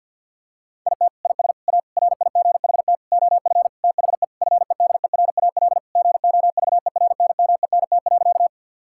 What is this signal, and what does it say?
Morse code, heard as a keyed tone, about 50 wpm, 700 Hz. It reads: IT IS A RIGHT OF THE FEDERAL GOVERNMENT.
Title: Sentences from Top 1000 Words - Review Part 5 - 50wpm